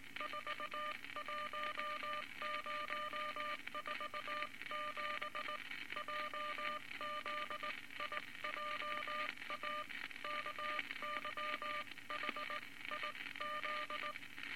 Morse code, heard as a keyed tone, about 20 wpm, spaced as for 17 wpm, 1200 Hz, with noise. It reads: 41047 JZIJA KYHIZ
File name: Morse Code Records / 19 B LCM 167 Twelve Words Per Minute